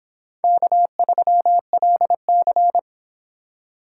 Morse code, heard as a keyed tone, about 30 wpm, 700 Hz. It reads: K3LC